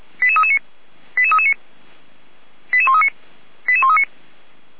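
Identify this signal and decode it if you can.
Five-tone selective call (ZVEI), heard as quick runs of five tones, tones of about 70 ms. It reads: 88339 88339 88139 88139
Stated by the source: ZVEI five-tone selective call 88190 received by SDR.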